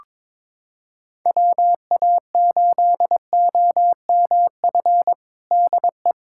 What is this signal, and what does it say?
Morse code, heard as a keyed tone, about 21 wpm, 700 Hz. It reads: WA8OMF DE
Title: USA Callsign DE Callsign - 6 Characters - 22wpm